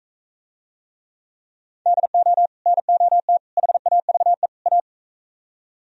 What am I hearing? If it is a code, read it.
Morse code, heard as a keyed tone, about 41 wpm, 700 Hz. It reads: DO NOT HAVE A